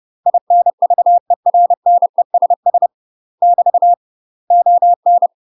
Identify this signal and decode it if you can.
Morse code, heard as a keyed tone, about 30 wpm, 700 Hz. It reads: INVERNESS <BT> ON
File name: UK MASTER QSOs TRAINING / 30WPM / UK MASTER QSOs 0034